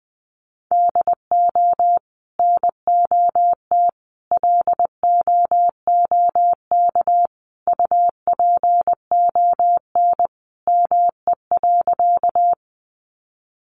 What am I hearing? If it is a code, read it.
Morse code, heard as a keyed tone, about 20 wpm, 700 Hz. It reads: DO NOT LOOK UPON ME.
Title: Sentences from Top 1100 Words 20wpm